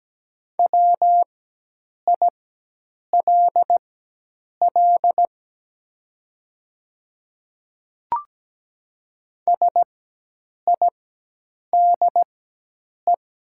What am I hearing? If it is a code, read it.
Morse code, heard as a keyed tone, about 17 wpm, spaced as for 8 wpm, 700 Hz. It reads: WILL SIDE
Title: 4x Character Spacing - Top 200 Words - 17wpm